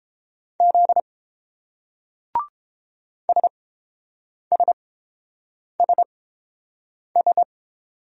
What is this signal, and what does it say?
Morse code, heard as a keyed tone, about 30 wpm, spaced as for 7 wpm, 690 Hz. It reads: Z SSSS